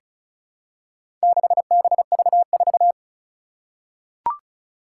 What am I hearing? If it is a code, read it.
Morse code, heard as a keyed tone, about 36 wpm, 700 Hz. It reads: 6BV4
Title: Four Challenging Letter-Number Combinations 35wpm